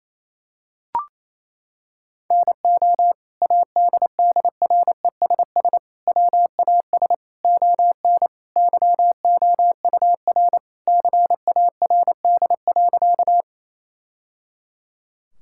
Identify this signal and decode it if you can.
Morse code, heard as a keyed tone, about 28 wpm, 700 Hz. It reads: NO ADDRESS WAS ON YOUR CARD.
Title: Sentences from Top 1000 Words - Review Part 6 - 28wpm